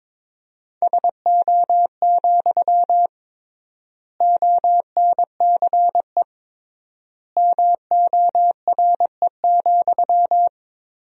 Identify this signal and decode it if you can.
Morse code, heard as a keyed tone, about 22 wpm, 700 Hz. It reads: SO, ONCE MORE,
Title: 3x Word-Spacing - Sentences from Top 300 Words - 22wpm